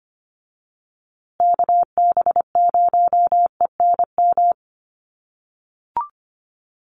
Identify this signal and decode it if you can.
Morse code, heard as a keyed tone, about 25 wpm, 700 Hz. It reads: KB0ENM
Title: USA Call Signs Repeated 2x 25wpm